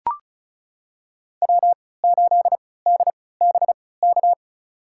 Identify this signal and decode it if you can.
Morse code, heard as a keyed tone, about 36 wpm, spaced as for 20 wpm, 690 Hz. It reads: W8DBK
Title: USA Call Signs - 5 Characters - 3x Character Spacing - 35wpm